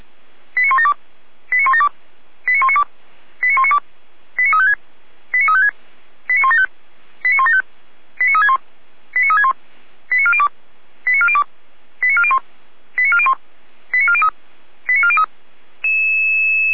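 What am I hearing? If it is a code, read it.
Five-tone selective call (ZVEI), heard as quick runs of five tones, tones of about 70 ms. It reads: 89172 89172 89192 89192 89356 89356 89175 89175 89361 89361 89402 89402 89401 89401 89403 89403
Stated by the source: Five-tone ZVEI selective call received on an SDR.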